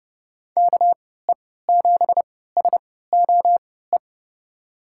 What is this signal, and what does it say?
Morse code, heard as a keyed tone, about 30 wpm, spaced as for 17 wpm, 700 Hz. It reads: KE7SOE
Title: USA Call Signs - 6 Characters - 3x Character Spacing - 30wpm